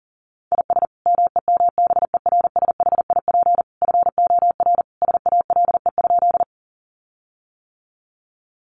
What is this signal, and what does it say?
Morse code, heard as a keyed tone, about 40 wpm, 700 Hz. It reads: IS MEMBERSHIP FOR SALE?
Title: Sentences from Top 1400 Words 40wpm